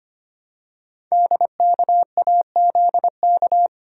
Morse code, heard as a keyed tone, about 25 wpm, 700 Hz. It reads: DKAZK
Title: Code Groups - 5 Letters - 25wpm